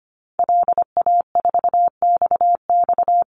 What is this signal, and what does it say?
Morse code, heard as a keyed tone, about 25 wpm, 700 Hz. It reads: LA4XX
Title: Call Signs - 25wpm